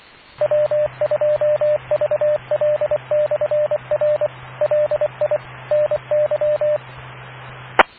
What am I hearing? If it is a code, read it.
Morse code, heard as a keyed tone, about 24 wpm, 600 Hz, with radio noise. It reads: W2VL/R LI NY E